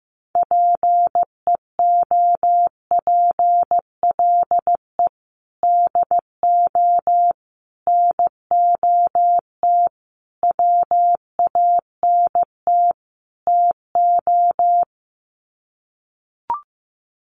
Morse code, heard as a keyed tone, about 15 wpm, 700 Hz. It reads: PEOPLE DO NOT WANT TO
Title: Sets of 5 Words Encore2 15wpm